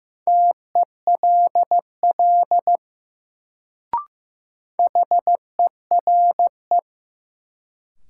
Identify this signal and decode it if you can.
Morse code, heard as a keyed tone, about 15 wpm, 700 Hz. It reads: TELL HERE
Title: Sentences from Top 200 Words - New Word Drill - 15wpm